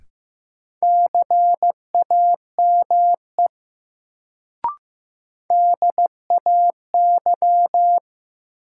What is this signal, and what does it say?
Morse code, heard as a keyed tone, about 15 wpm, 700 Hz. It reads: CAME DAY
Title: Sentences from Top 200 Words - New Word Drill - 15wpm